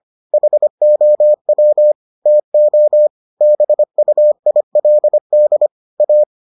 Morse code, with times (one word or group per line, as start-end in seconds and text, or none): 0.3-1.9 HOW
2.3-3.1 TO
3.4-5.7 BUILD
6.0-6.2 A